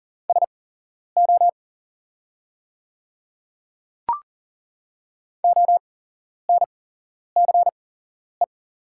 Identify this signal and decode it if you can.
Morse code, heard as a keyed tone, about 40 wpm, spaced as for 11 wpm, 690 Hz. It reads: SO ONCE
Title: 8x Character Spacing - Top 300 Words - 40wpm